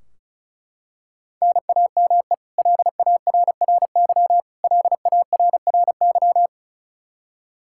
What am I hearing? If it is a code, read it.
Morse code, heard as a keyed tone, about 34 wpm, 700 Hz. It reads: NAME LARRY LARRY
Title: Practice QSO - Exchange 1 - Beginning - 35wpm